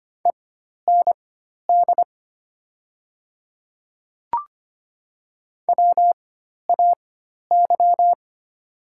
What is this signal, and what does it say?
Morse code, heard as a keyed tone, about 25 wpm, spaced as for 12 wpm, 700 Hz. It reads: END WAY
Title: 4x Character Spacing - Top 200 Words - 25wpm